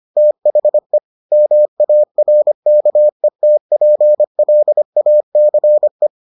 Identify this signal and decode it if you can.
Morse code, heard as a keyed tone, about 25 wpm, 600 Hz. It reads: THE MARKETPLACE